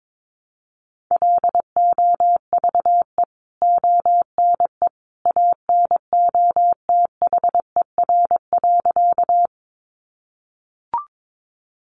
Morse code, heard as a keyed tone, about 22 wpm, 700 Hz. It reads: LOVE ONE ANOTHER.
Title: Sentences from Top 600 Words 22wpm